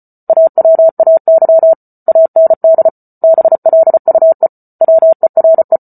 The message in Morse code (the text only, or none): AWAY AND BLUE WERE